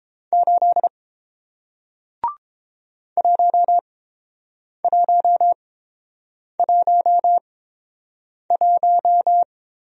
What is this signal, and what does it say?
Morse code, heard as a keyed tone, about 29 wpm, spaced as for 7 wpm, 700 Hz. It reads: 8 1111